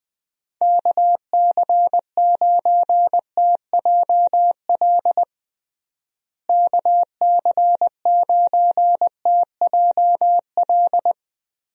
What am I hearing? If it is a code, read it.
Morse code, heard as a keyed tone, about 20 wpm, 700 Hz. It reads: KC9TJL KC9TJL
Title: USA Call Signs - 6 Characters - Repeated 3x - 20wpm